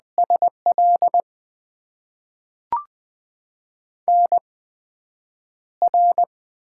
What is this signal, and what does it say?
Morse code, heard as a keyed tone, about 20 wpm, spaced as for 9 wpm, 700 Hz. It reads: SL NR
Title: Top 90 QSO Elements - 8x Character Spacing - 20wpm